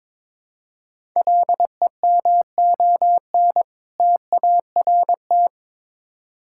Morse code, heard as a keyed tone, about 22 wpm, 700 Hz. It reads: LEMON TART
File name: Difficult Words with Hints - 22wpm